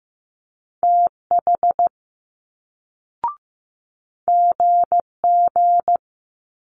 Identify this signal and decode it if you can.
Morse code, heard as a keyed tone, about 15 wpm, 700 Hz. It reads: TH GG
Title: Multi-letter Phonemes Quick Fire 15wpm